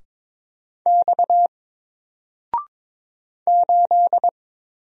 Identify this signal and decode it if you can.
Morse code, heard as a keyed tone, about 22 wpm, 700 Hz. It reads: X 8
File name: Single-Letter-Number-22wpm